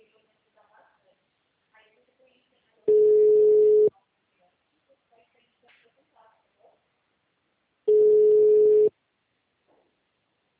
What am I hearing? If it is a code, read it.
Ringback tone (Europe, 425 Hz), with speech.